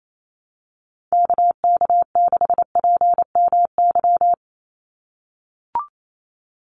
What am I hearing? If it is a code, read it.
Morse code, heard as a keyed tone, about 28 wpm, 700 Hz. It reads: KK6PMY
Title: USA Call Signs 28wpm